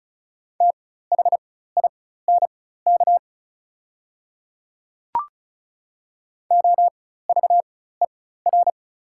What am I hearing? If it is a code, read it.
Morse code, heard as a keyed tone, about 34 wpm, spaced as for 16 wpm, 690 Hz. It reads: THINK OVER
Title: 4x Character Spacing - Top 200 Words - 35wpm